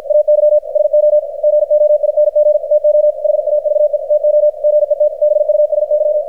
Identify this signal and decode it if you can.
Morse code, heard as a keyed tone, about 53 wpm, 600 Hz, with radio noise. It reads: AY SO GOING TO HAVE TO GET BUSY
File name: archived / CRSnet